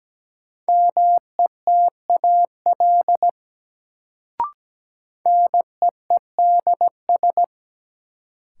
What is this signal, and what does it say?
Morse code, heard as a keyed tone, about 17 wpm, 700 Hz. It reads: METAL NEEDS
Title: Sentences from Top 1600 Words - New Word Drill Alt - 17wpm